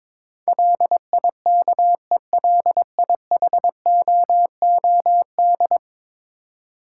Morse code, heard as a keyed tone, about 22 wpm, 700 Hz. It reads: LIKELIHOOD
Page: morse-code-ninja-practice-22wpm